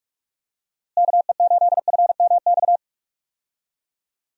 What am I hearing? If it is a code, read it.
Morse code, heard as a keyed tone, about 41 wpm, 700 Hz. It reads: KE8FMX
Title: USA Call Signs - 6 Characters - Repeated 3x - 45wpm